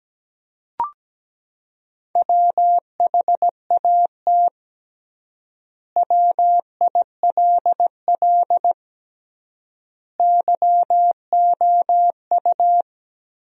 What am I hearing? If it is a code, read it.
Morse code, heard as a keyed tone, about 17 wpm, 700 Hz. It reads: WHAT WILL YOU
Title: 3x Word-Spacing - Sentences from Top 100 Words - 17wpm